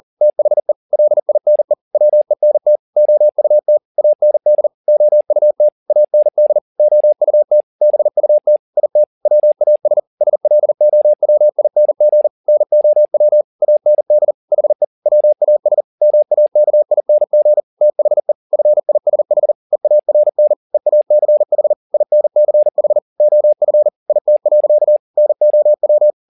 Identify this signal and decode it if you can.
Morse code, heard as a keyed tone, about 40 wpm, 600 Hz. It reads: THE LINE WENT OUT AND OUT AND OUT BUT IT WAS SLOWING NOW AND HE WAS MAKING THE FISH EARN EACH INCH OF IT. NOW
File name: the old man and the sea 2 40wpm